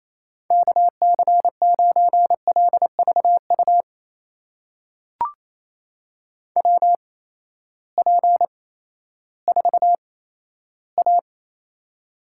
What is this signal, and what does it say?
Morse code, heard as a keyed tone, about 28 wpm, spaced as for 14 wpm, 700 Hz. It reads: KC9LVU WP4A
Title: USA Call Signs - 6 Characters - 8x Character Spacing - 28wpm